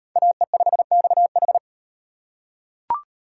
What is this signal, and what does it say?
Morse code, heard as a keyed tone, about 37 wpm, 700 Hz. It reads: AE5XH